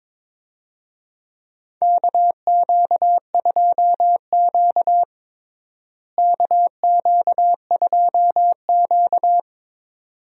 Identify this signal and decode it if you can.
Morse code, heard as a keyed tone, about 22 wpm, 700 Hz. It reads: KQ2Q KQ2Q
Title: USA Call Signs - 4 Characters - Repeated 2x - 22wpm